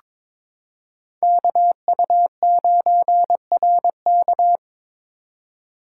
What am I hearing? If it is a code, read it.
Morse code, heard as a keyed tone, about 22 wpm, 700 Hz. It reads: KU9RK
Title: K1USN SST QSO Practice - 22wpm